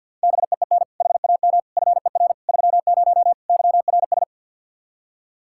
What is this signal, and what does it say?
Morse code, heard as a keyed tone, about 52 wpm, 700 Hz. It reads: BEEN HAM FER 30 YRS
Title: Practice QSO - Exchange 3 - More Info - 50wpm